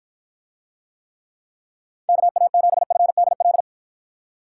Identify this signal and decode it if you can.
Morse code, heard as a keyed tone, about 53 wpm, 700 Hz. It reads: KA7FDL